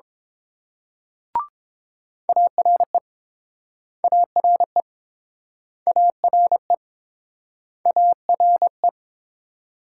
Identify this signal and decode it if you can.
Morse code, heard as a keyed tone, about 30 wpm, 700 Hz. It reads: ARE ARE ARE ARE